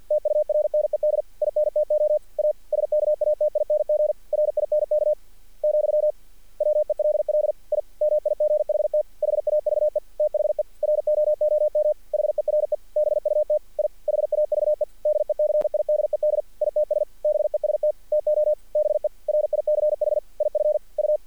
Morse code, heard as a keyed tone, about 48 wpm, 600 Hz. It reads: TURNED INTO A SKATING RINK , WELL I MIGHT HAVE THE ROOM HERE BUT I HAVE DECIDED ITS BEST TO BE RICH IF U
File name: archived / CRSnet